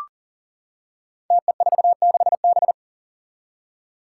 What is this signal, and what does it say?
Morse code, heard as a keyed tone, about 40 wpm, 700 Hz. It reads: TE46B
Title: Five Challenging Letter-Number Combinations 40wpm